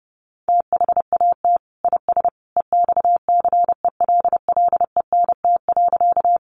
Morse code, heard as a keyed tone, about 30 wpm, 700 Hz. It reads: THAT IS EXCELLENT.
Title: Sentences from Top 1000 Words 30wpm